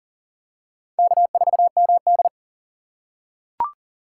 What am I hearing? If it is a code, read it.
Morse code, heard as a keyed tone, about 40 wpm, 700 Hz. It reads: K4MD
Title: USA Call Signs - 2x Character Spacing - 40wpm